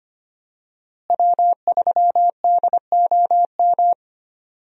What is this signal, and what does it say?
Morse code, heard as a keyed tone, about 25 wpm, 700 Hz. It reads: W3DOM
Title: USA Call Signs - 5 Characters - Repeated 2x - 25wpm